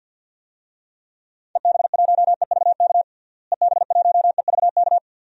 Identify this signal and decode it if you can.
Morse code, heard as a keyed tone, about 52 wpm, 700 Hz. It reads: EB1EVK EB1EVK
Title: DXCC-100 Call Signs - 6 Characters - Repeated 2x - 50wpm